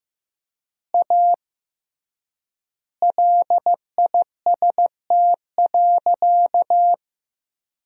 Morse code, heard as a keyed tone, about 15 wpm, 700 Hz. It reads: A LIST.
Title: 3x Word-Spacing - Sentences from Top 400 Words - 15wpm